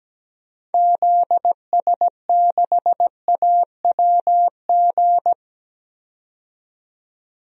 Morse code, heard as a keyed tone, about 17 wpm, 700 Hz. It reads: ZS6AWG
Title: DXCC-100 Call Signs - 6 Characters - 17wpm